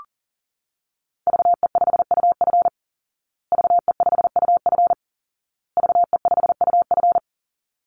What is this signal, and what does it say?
Morse code, heard as a keyed tone, about 40 wpm, 700 Hz. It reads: VE5UF VE5UF VE5UF
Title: Call Signs Repeated 2x - 40wpm